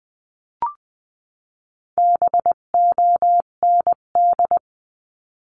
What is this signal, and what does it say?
Morse code, heard as a keyed tone, about 20 wpm, 700 Hz. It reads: BOND
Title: 258 Lesson 42c - Words 16-20wpm